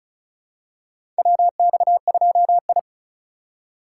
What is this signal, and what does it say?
Morse code, heard as a keyed tone, about 34 wpm, 700 Hz. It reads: WX2I
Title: USA Call Signs - 4 Characters - 4x Character Spacing - 35wpm